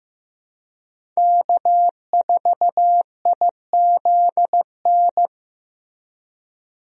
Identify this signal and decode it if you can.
Morse code, heard as a keyed tone, about 15 wpm, 700 Hz. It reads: K4IZN
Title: USA Call Signs - 5 Characters - Repeated 2x - 15wpm